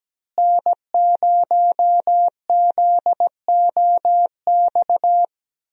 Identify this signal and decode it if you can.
Morse code, heard as a keyed tone, about 17 wpm, 700 Hz. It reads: N0ZOX
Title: USA Call Signs - 5 Characters - Repeated 2x - 17wpm